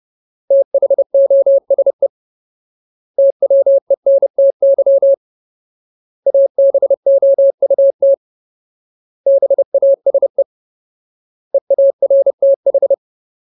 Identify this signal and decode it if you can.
Morse code, heard as a keyed tone, about 30 wpm, 550 Hz. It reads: THOSE TWENTY ABOUT BASE EARTH
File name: Ultimate Word List / 30WPM Ultimate